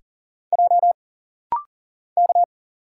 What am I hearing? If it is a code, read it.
Morse code, heard as a keyed tone, about 42 wpm, 700 Hz. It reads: J K
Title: Single Letter-Number - Rapid-Fire 40wpm